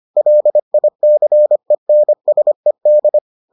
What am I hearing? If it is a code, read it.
Morse code, heard as a keyed tone, about 25 wpm, 600 Hz. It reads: LICENSED